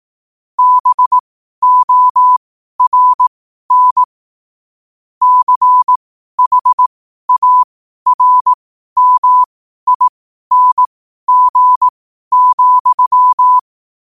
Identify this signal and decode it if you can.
Morse code, heard as a keyed tone, about 18 wpm, spaced as for 12 wpm, 1000 Hz. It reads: BORN CHARMING,
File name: qotdfiles / 12wpm